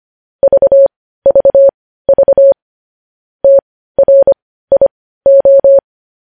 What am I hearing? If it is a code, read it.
Morse code, heard as a keyed tone, about 25 wpm, spaced as for 15 wpm, 550 Hz. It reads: VVV TRIO
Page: news headlines in morse code at 15 wpm